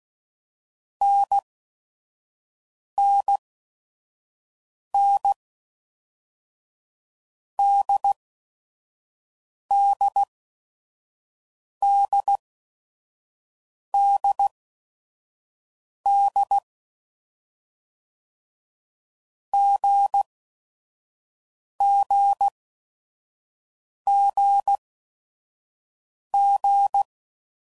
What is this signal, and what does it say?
Morse code, heard as a keyed tone, about 16 wpm, 800 Hz. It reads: N N N D D D D D G G G G
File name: Basic Letter Groups By Blocks / Groups@5 / AWUNDG@5 f15